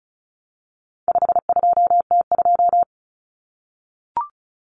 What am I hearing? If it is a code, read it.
Morse code, heard as a keyed tone, about 34 wpm, 700 Hz. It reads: 52T2
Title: Common Cut Numbers - 4-Digit 35wpm